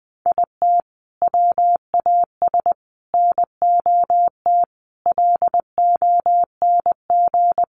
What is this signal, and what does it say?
Morse code, heard as a keyed tone, about 20 wpm, 700 Hz. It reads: IT WAS NOT LONG
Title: Sets of 5 Words Encore2 20wpm